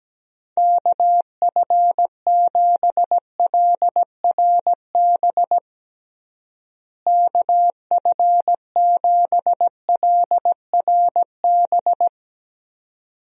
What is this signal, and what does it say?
Morse code, heard as a keyed tone, about 17 wpm, 700 Hz. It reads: KF7LRB KF7LRB
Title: USA Call Signs - 6 Characters - Repeated 2x - 17wpm